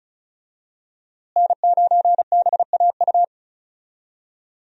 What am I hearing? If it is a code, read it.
Morse code, heard as a keyed tone, about 36 wpm, 700 Hz. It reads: N9BAU
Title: USA Call Signs - 5 Characters - Repeated 3x - 35wpm